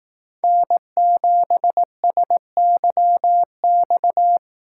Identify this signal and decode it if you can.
Morse code, heard as a keyed tone, about 18 wpm, 700 Hz. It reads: N7SYX
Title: Speed Racing - USA 1x3 Call Signs 15wpm